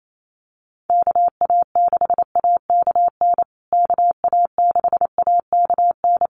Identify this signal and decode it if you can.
Morse code, heard as a keyed tone, about 28 wpm, 700 Hz. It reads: KA6AKN KA6AKN
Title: USA Call Signs Repeated 3x 28wpm